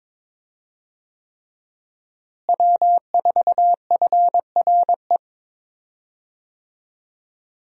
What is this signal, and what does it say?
Morse code, heard as a keyed tone, about 22 wpm, 700 Hz. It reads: W4FRE
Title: USA Call Signs - 5 Characters - Repeated 3x - 22wpm